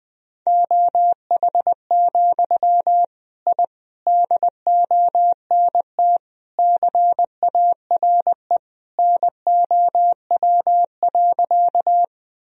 Morse code, heard as a keyed tone, about 20 wpm, 700 Hz. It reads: OH, I DONT CARE NOW.
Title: Sentences from Top 1000 Words - Review Part 1 - 20wpm